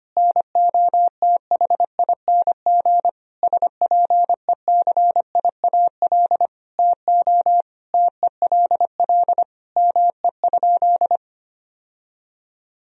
Morse code, heard as a keyed tone, about 25 wpm, 700 Hz. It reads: NOTHING SPECIAL TO TELL ME?
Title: Sentences from Top 1000 Words - Review Part 3 - 25wpm